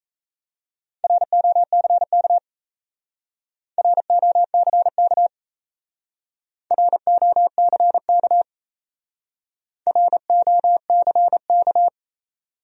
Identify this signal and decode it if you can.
Morse code, heard as a keyed tone, about 37 wpm, 700 Hz. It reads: ROCK ROCK ROCK ROCK